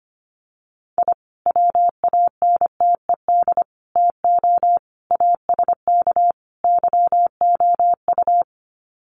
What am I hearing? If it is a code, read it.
Morse code, heard as a keyed tone, about 25 wpm, 700 Hz. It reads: I WANTED TO ASK YOU
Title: Sets of 5 Words Encore2 25wpm